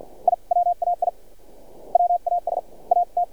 Morse code, heard as a keyed tone, about 49 wpm, spaced as for 43 wpm, 700 Hz. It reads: I WAI WAS AT
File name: archived / CRSnet